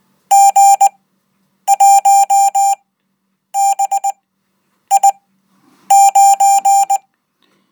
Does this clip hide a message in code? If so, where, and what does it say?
Yes: Morse code from 0.3 to 7.0 s, heard as a keyed tone, about 19 wpm, spaced as for 8 wpm, 790 Hz. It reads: G1BI9